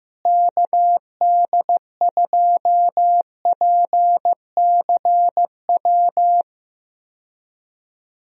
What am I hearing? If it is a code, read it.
Morse code, heard as a keyed tone, about 16 wpm, 700 Hz. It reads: KD2PCW
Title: USA Call Signs - 6 Characters - Repeated 2x - 15wpm